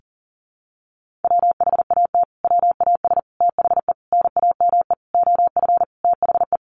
Morse code, heard as a keyed tone, about 40 wpm, 700 Hz. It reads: WHAT WAS THE NAME OF THE
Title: Sentences from Top 300 Words 40wpm